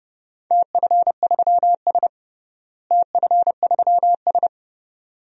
Morse code, heard as a keyed tone, about 30 wpm, 700 Hz. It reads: TF3S TF3S
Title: DXCC-100 Call Signs - 4 Characters - Repeated 2x - 30wpm